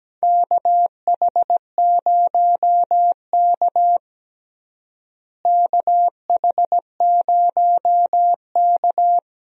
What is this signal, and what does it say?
Morse code, heard as a keyed tone, about 17 wpm, 700 Hz. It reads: KH0K KH0K
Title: USA Call Signs - 4 Characters - Repeated 2x - 17wpm